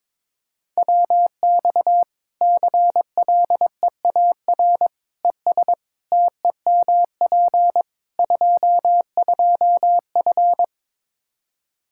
Morse code, heard as a keyed tone, about 22 wpm, 700 Hz. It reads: WX CLEAR ES TEMP 22F
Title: Practice QSO - Exchange 2 - Info - 22wpm